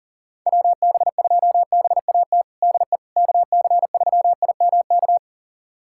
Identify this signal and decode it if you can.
Morse code, heard as a keyed tone, about 40 wpm, 700 Hz. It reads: WB2BAT DE KC3IMK
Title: USA Callsign DE Callsign - 6 Characters - 40wpm